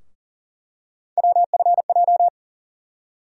Morse code, heard as a keyed tone, about 40 wpm, 700 Hz. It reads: WFJ
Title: USA Call Sign Postfixes - 3 Letters 40wpm